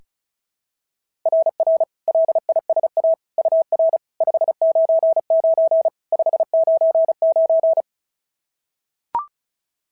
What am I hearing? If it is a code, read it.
Morse code, heard as a keyed tone, about 33 wpm, 660 Hz. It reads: RR LISA UR 599 599
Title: SOTA QSO Practice 35wpm